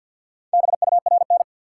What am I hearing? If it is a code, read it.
Morse code, heard as a keyed tone, about 53 wpm, 700 Hz. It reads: BURN
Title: Sentences from Top 1300 Words - New Word Drill Alt - 50wpm